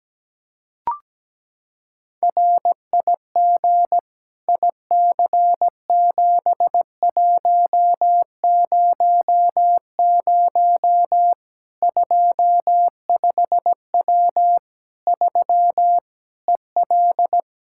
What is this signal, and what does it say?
Morse code, heard as a keyed tone, about 17 wpm, 700 Hz. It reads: RIG IC7100 25W 3 EL
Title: Practice QSO - Exchange 2 - Info - 17wpm